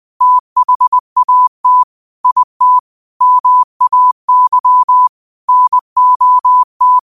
Morse code, heard as a keyed tone, about 20 wpm, 1000 Hz. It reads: THAT IT MAY NOT